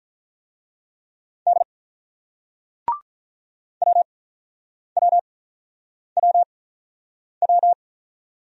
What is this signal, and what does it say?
Morse code, heard as a keyed tone, about 46 wpm, spaced as for 9 wpm, 700 Hz. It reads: D WWWW